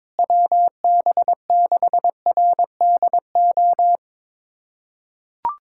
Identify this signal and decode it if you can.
Morse code, heard as a keyed tone, about 22 wpm, 700 Hz. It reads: WB6RDO
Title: USA Callsign DE Callsign 22wpm